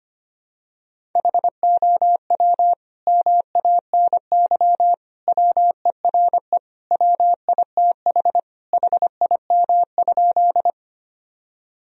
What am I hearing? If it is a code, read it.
Morse code, heard as a keyed tone, about 25 wpm, 700 Hz. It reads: HOW MANY WERE WITH HIM?
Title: Sentences from Top 1000 Words - Review Part 1 - 25wpm